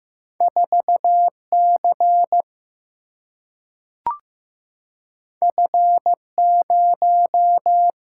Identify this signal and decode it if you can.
Morse code, heard as a keyed tone, about 15 wpm, 700 Hz. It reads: 4 C F 0
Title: Single-Letter Single-Number 15wpm